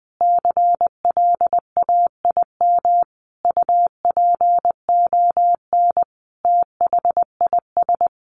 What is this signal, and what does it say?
Morse code, heard as a keyed tone, about 20 wpm, 700 Hz. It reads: CLAIM UPON THIS
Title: Sentences from Top 1500 Words 20wpm